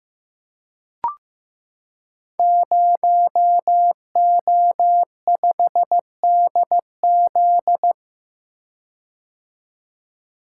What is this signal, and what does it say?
Morse code, heard as a keyed tone, about 16 wpm, 700 Hz. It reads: 0O5DZ
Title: Code Groups - 5 Letter:Numbers - 15wpm